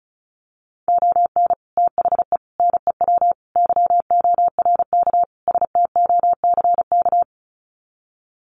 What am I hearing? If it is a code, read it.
Morse code, heard as a keyed tone, about 34 wpm, 700 Hz. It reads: ON THE NEW YORK STOCK
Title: Sets of 5 Words Encore 35wpm